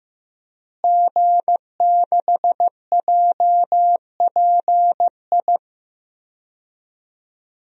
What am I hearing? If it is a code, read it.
Morse code, heard as a keyed tone, about 15 wpm, 700 Hz. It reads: G6JPI
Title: DXCC-100 Call Signs - 5 Characters - Repeated 3x - 15wpm